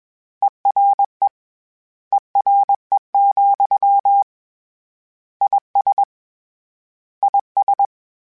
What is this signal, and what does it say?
Morse code, heard as a keyed tone, about 21 wpm, 800 Hz. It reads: ERE ERE, IS IS
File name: General / General-mp3s / General @20